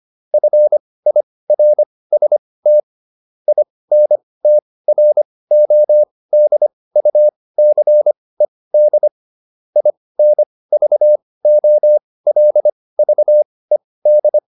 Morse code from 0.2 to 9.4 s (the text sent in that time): FIRST INTRODUCED